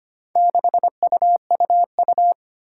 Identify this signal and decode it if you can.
Morse code, heard as a keyed tone, about 25 wpm, 700 Hz. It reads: 6UUU
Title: Four Challenging Letter-Number Combinations 25wpm